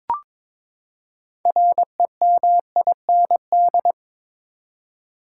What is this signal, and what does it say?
Morse code, heard as a keyed tone, about 21 wpm, 700 Hz. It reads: REMIND
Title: Six Letter Words - 22wpm